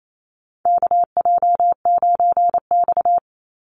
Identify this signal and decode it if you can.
Morse code, heard as a keyed tone, about 28 wpm, 700 Hz. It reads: KJ9X
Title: NA SOTA Chaser Callsigns 28wpm